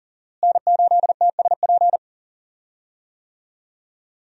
Morse code, heard as a keyed tone, about 40 wpm, 700 Hz. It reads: N8TSP
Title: USA Call Signs - 5 Characters - Repeated 2x - 40wpm